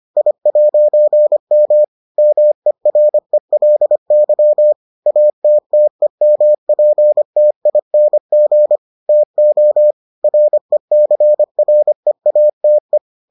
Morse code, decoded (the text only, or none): I'M MERELY ATTEMPTING TO RECREATE